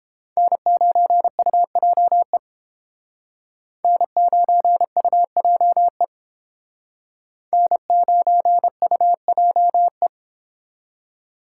Morse code, heard as a keyed tone, about 30 wpm, 700 Hz. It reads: N9UJE N9UJE N9UJE